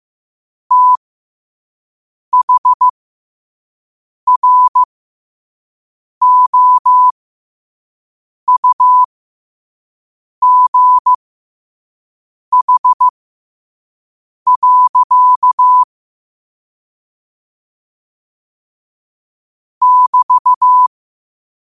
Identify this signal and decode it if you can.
Morse code, heard as a keyed tone, about 15 wpm, spaced as for 5 wpm, 1000 Hz. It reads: THROUGH. <BT>
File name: qotdfiles / 5wpm